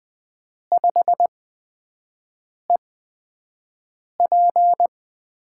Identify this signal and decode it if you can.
Morse code, heard as a keyed tone, about 20 wpm, spaced as for 5 wpm, 700 Hz. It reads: 5EP
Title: USA Call Signs - 4 Characters - 8x Character Spacing - 20wpm